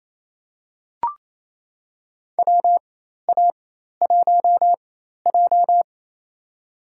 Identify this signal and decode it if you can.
Morse code, heard as a keyed tone, about 28 wpm, spaced as for 13 wpm, 700 Hz. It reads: WA1J
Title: USA Call Signs - 4x Character Spacing - 28wpm